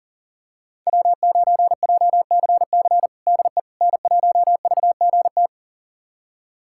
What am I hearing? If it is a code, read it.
Morse code, heard as a keyed tone, about 40 wpm, 700 Hz. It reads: W9JCC DE N1VGT
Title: USA Callsign DE Callsign - 5 Characters - 40wpm